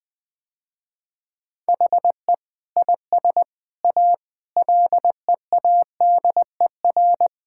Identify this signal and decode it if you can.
Morse code, heard as a keyed tone, about 20 wpm, 700 Hz. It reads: HE IS A LEADER
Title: Sentences with 6-Word Phrases - Spoken Beginning - 20wpm